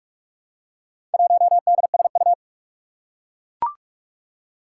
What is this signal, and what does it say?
Morse code, heard as a keyed tone, about 44 wpm, 700 Hz. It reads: 1DSU
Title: Four Challenging Letter-Number Combinations 45wpm